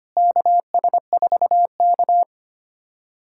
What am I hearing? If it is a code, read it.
Morse code, heard as a keyed tone, about 25 wpm, 700 Hz. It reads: KS4K
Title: USA Callsign DE Callsign - 4 Characters - 25wpm